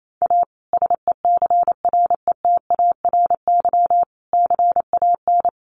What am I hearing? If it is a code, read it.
Morse code, heard as a keyed tone, about 28 wpm, 700 Hz. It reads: A SECRETARY CAN
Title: Sentences from Top 1300 Words 28wpm